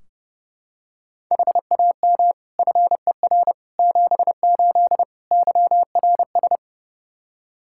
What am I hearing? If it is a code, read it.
Morse code, heard as a keyed tone, about 30 wpm, 700 Hz. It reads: HAM FER 78 YRS
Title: Practice QSO - Exchange 3 - More Info - 30wpm